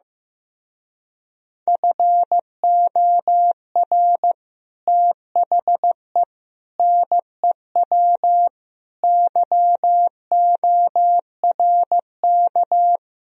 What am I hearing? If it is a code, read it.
Morse code, heard as a keyed tone, about 15 wpm, 700 Hz. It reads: FOR THE NEW YORK
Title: Speed-Racing - Sets of 5 Words 15wpm